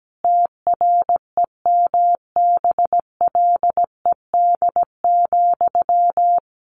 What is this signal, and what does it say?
Morse code, heard as a keyed tone, about 17 wpm, 700 Hz. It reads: TREMBLED,
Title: A Sound Of Thunder 17wpm